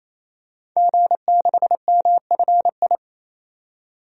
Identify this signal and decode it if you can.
Morse code, heard as a keyed tone, about 27 wpm, 690 Hz. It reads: G6MFI
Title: DXCC-100 Call Signs - 5 Characters - 28wpm